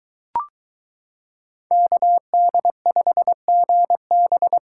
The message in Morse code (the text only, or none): KD5GB